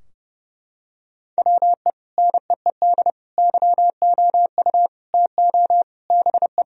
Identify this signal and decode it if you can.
Morse code, heard as a keyed tone, about 32 wpm, 700 Hz. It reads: WE NEED YOU TO BE
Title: Sentences from Top 1000 Words - Review Part 3 - 30wpm